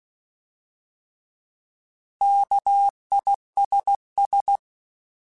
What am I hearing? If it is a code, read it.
Morse code, heard as a keyed tone, about 16 wpm, 800 Hz. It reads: KISS
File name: Alphabetical / Alphabetical mp3s / Ks @15